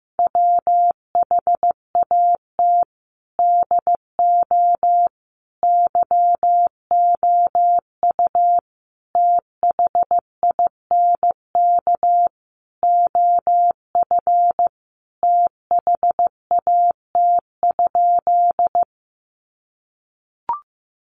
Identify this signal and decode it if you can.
Morse code, heard as a keyed tone, about 15 wpm, 700 Hz. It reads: WHAT DO YOU THINK OF THAT?
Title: Sentences from Top 200 Words 15wpm